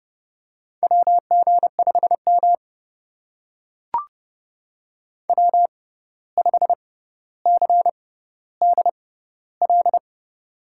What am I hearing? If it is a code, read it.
Morse code, heard as a keyed tone, about 30 wpm, spaced as for 14 wpm, 700 Hz. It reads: WG5M W5CDL
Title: USA Call Signs - 6x Character Spacing - 30wpm